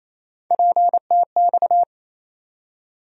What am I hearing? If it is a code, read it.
Morse code, heard as a keyed tone, about 28 wpm, 700 Hz. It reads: PTX
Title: USA Call Sign Postfixes - 3 Letters 28wpm